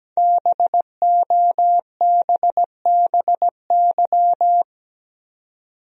Morse code, heard as a keyed tone, about 17 wpm, 700 Hz. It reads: BOBBY
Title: USA Common Names - 17wpm